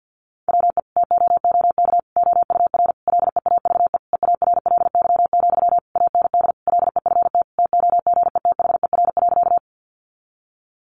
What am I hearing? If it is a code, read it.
Morse code, heard as a keyed tone, about 50 wpm, 700 Hz. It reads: WE TOOK OUR LEAVE EARLY, AND LEFT TOGETHER.
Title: Sentences from Top 400 Words 50wpm